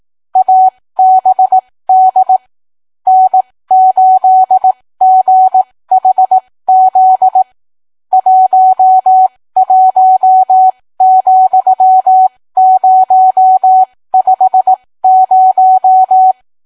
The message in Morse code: ABD N8GHZ 11,050